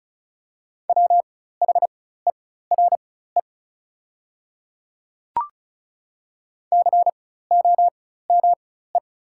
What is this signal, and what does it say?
Morse code, heard as a keyed tone, about 36 wpm, spaced as for 16 wpm, 690 Hz. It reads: WHERE COME
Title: 4x Character Spacing - Top 300 Words - 35wpm